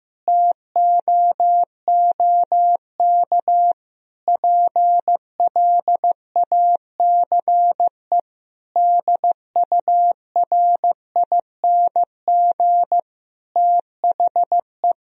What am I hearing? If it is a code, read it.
Morse code, heard as a keyed tone, about 15 wpm, 700 Hz. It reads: TOOK PLACE DURING THE
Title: Sentences from Top 1000 Words - Review Part 4 - 15wpm